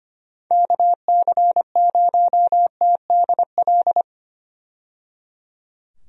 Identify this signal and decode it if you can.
Morse code, heard as a keyed tone, about 25 wpm, 700 Hz. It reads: KC0TDL
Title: USA Call Signs - 6 Characters - 25wpm